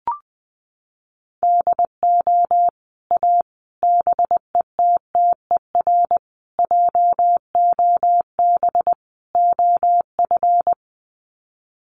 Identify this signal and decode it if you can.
Morse code, heard as a keyed tone, about 20 wpm, 700 Hz. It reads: DO A BETTER JOB OF
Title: Sets of 5 Words Encore2 20wpm